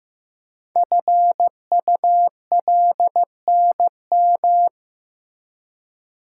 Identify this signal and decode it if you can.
Morse code, heard as a keyed tone, about 15 wpm, 700 Hz. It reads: FULNM
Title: Code Groups - 5 Letters - 15wpm